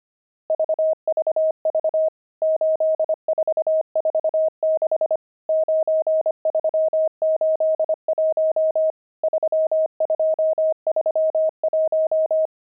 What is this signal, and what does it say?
Morse code, heard as a keyed tone, about 25 wpm, 630 Hz. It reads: VVV 8446 9381 3231
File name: pangram-fs-25-cs-25-nums14